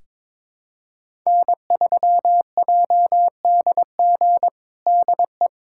Morse code, heard as a keyed tone, about 22 wpm, 700 Hz. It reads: N3JDG DE
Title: USA Callsign DE Callsign 22wpm